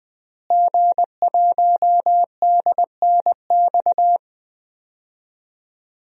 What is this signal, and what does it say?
Morse code, heard as a keyed tone, about 20 wpm, 700 Hz. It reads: G1DNX
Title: DXCC-100 Call Signs - 5 Characters - Repeated 3x - 20wpm